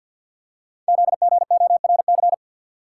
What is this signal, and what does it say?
Morse code, heard as a keyed tone, about 48 wpm, 700 Hz. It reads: ZGORC
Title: Code Groups - 5 Letters - 50wpm